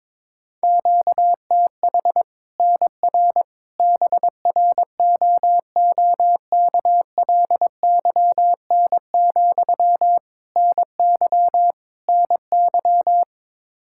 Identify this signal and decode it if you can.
Morse code, heard as a keyed tone, about 22 wpm, 700 Hz. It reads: QTH NR BROOKLYN, NY NY
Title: Practice QSO - Exchange 1 - Beginning - 22wpm